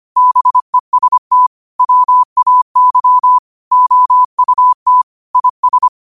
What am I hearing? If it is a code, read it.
Morse code, heard as a keyed tone, about 25 wpm, 1000 Hz. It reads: BEST WAY OUT IS
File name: qotdfiles / 25wpm